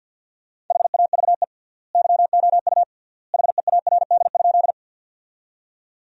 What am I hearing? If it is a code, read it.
Morse code, heard as a keyed tone, about 50 wpm, 700 Hz. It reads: HAVE YOU HEARD?
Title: 3x Word-Spacing - Sentences from Top 400 Words - 50wpm